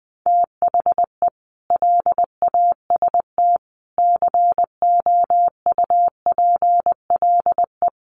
Morse code, heard as a keyed tone, about 20 wpm, 700 Hz. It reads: THE LAST COUPLE
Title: Sets of 5 Words Encore2 20wpm